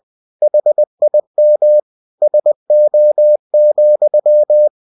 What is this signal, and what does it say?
Morse code, heard as a keyed tone, about 20 wpm, 600 Hz. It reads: HIM SO,
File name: cw-books / winnie-the-pooh / winnie-20wpm-page-0097